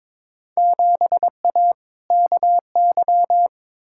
Morse code, heard as a keyed tone, about 21 wpm, 700 Hz. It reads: 7A KY
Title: Field Day Exchanges Encore - 22wpm